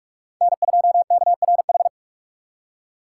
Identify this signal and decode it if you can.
Morse code, heard as a keyed tone, about 46 wpm, 700 Hz. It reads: N2KRH
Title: USA Call Signs - 4x Character Spacing - 45wpm